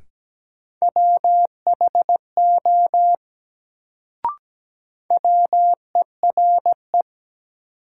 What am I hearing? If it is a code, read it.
Morse code, heard as a keyed tone, about 17 wpm, 700 Hz. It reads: WHO WERE
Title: Sentences from Top 100 Words - New Word Drill - 17wpm